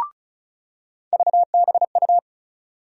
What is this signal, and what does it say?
Morse code, heard as a keyed tone, about 34 wpm, 700 Hz. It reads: VBU
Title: USA Call Sign Postfixes - 3 Letters 35wpm